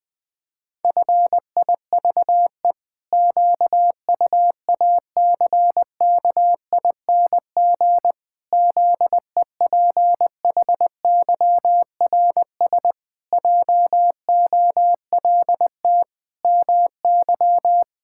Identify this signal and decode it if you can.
Morse code, heard as a keyed tone, about 20 wpm, 700 Hz. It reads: FIVE QUACKING ZEPHYRS JOLT MY
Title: Pangram Challenge v2 - 20wpm